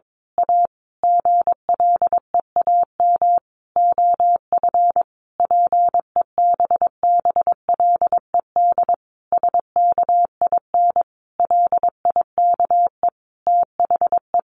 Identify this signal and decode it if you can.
Morse code, heard as a keyed tone, about 22 wpm, 700 Hz. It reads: A GLEAM OF PEBBLED SKIN LIKE THE
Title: A Sound Of Thunder 22wpm